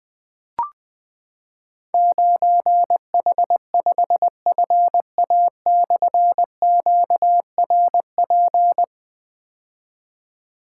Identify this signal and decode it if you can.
Morse code, heard as a keyed tone, about 20 wpm, 700 Hz. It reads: 9H5FA/QRP
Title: Difficult Call Signs 20wpm